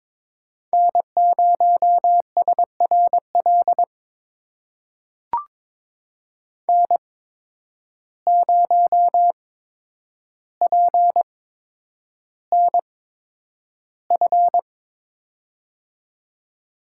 Morse code, heard as a keyed tone, about 22 wpm, spaced as for 9 wpm, 700 Hz. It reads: N0SRL N0PNF
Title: USA Call Signs - 5 Characters - 8x Character Spacing - 22wpm